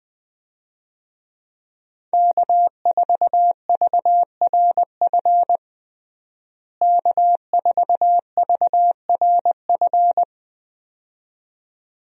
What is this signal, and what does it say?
Morse code, heard as a keyed tone, about 21 wpm, 700 Hz. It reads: K4VRF K4VRF
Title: USA Call Signs - 5 Characters - Repeated 2x - 20wpm